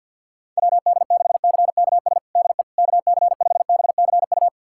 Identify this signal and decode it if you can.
Morse code, heard as a keyed tone, about 48 wpm, 700 Hz. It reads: WD6CKS DE KC5BCU
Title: USA Callsign DE Callsign - 6 Characters - 50wpm